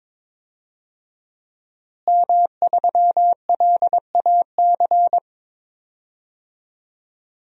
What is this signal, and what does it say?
Morse code, heard as a keyed tone, about 22 wpm, 700 Hz. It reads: M3LAC
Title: DXCC-100 Call Signs - 5 Characters - Repeated 3x - 22wpm